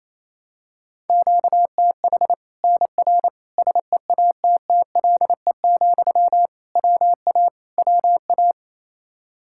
Morse code, heard as a keyed tone, about 28 wpm, 700 Hz. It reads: QTH NR SEATTLE, WA WA
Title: Practice QSO - Exchange 1 - Beginning - 28wpm